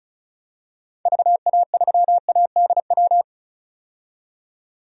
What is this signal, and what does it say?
Morse code, heard as a keyed tone, about 35 wpm, 700 Hz. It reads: VA3ADW
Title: DXCC-100 Call Signs - 6 Characters - 35wpm